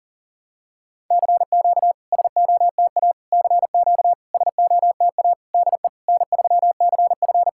Morse code, heard as a keyed tone, about 41 wpm, 700 Hz. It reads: CQ SOTA CQ SOTA DE N3CF
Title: SOTA QSO Practice 40wpm